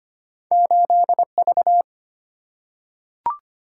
Morse code, heard as a keyed tone, about 25 wpm, 700 Hz. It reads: 8V
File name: Two Letter-Number Combinations - 25wpm